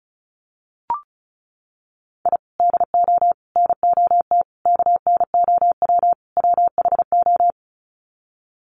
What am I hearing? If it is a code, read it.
Morse code, heard as a keyed tone, about 34 wpm, 700 Hz. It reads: I DO NOT KNOW WHO
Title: Sets of 5 Words 35wpm